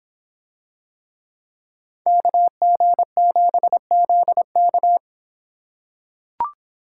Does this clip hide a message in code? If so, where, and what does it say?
Yes: Morse code from 2.1 to 5.0 s, heard as a keyed tone, about 26 wpm, 700 Hz. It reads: KG7ZK